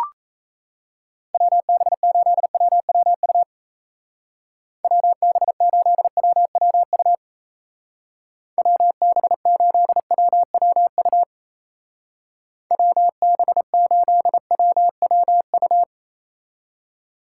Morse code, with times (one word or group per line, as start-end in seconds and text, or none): 1.3-3.4 WB8WWU
4.8-7.1 WB8WWU
8.6-11.2 WB8WWU
12.7-15.8 WB8WWU